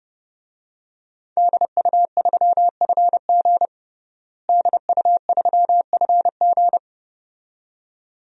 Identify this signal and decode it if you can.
Morse code, heard as a keyed tone, about 32 wpm, 700 Hz. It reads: DU3FG DU3FG
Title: DXCC-100 Call Signs - 5 Characters - Repeated 2x - 30wpm